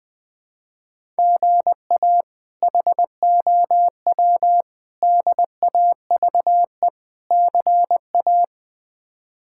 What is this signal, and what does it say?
Morse code, heard as a keyed tone, about 20 wpm, 700 Hz. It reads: GA HOW DAVE CA
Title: K1USN SST QSO Practice - 20wpm